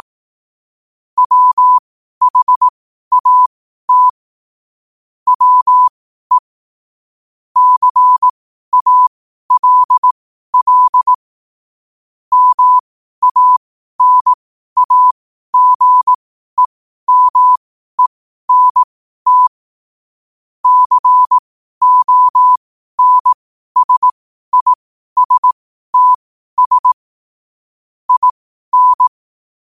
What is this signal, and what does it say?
Morse code, heard as a keyed tone, about 18 wpm, spaced as for 12 wpm, 1000 Hz. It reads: WHAT WE CALL MANAGEMENT CONSISTS IN
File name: qotdfiles / 12wpm